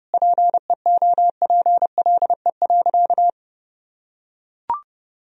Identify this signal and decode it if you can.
Morse code, heard as a keyed tone, about 30 wpm, 700 Hz. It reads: PEOPLE.
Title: Sentences from Top 1000 Words - Review Part 3 - 30wpm